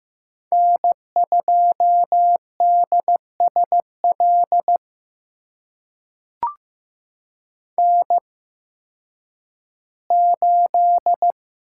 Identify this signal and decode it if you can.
Morse code, heard as a keyed tone, about 15 wpm, spaced as for 10 wpm, 700 Hz. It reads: N2DSL N8
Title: USA Call Signs - 8x Character Spacing - 15wpm